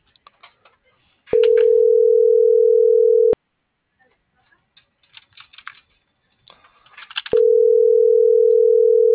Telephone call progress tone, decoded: ringback tone

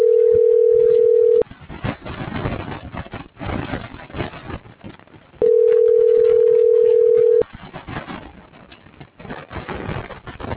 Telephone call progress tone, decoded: ringback tone

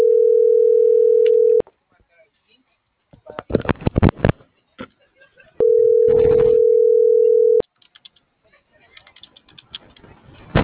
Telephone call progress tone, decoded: ringback tone

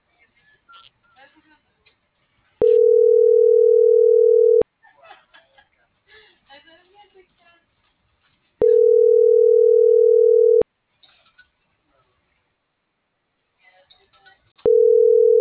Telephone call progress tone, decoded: ringback tone

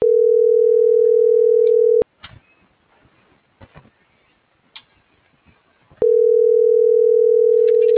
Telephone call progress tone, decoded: ringback tone

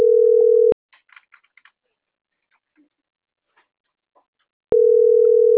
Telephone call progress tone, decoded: ringback tone

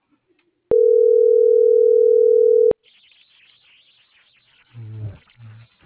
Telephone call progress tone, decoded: ringback tone

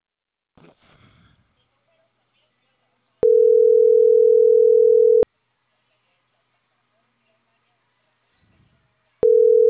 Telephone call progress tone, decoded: ringback tone

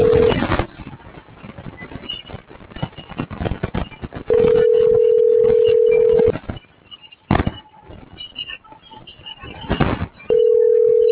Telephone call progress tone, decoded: ringback tone

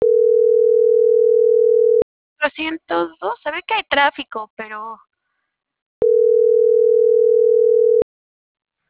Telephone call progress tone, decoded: ringback tone

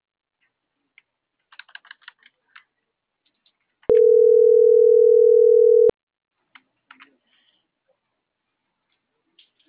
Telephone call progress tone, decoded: ringback tone

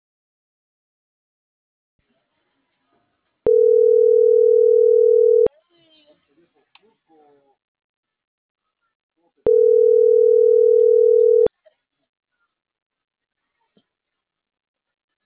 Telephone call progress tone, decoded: ringback tone

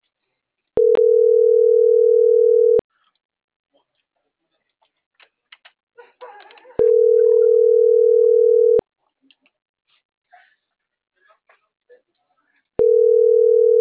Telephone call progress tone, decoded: ringback tone

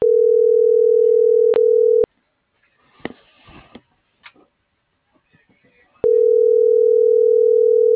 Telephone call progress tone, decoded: ringback tone

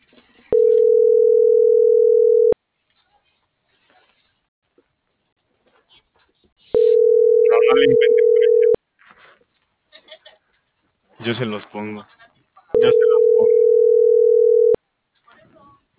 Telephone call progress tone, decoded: ringback tone

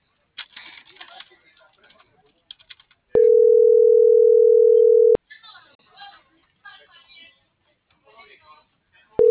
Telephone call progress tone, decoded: ringback tone

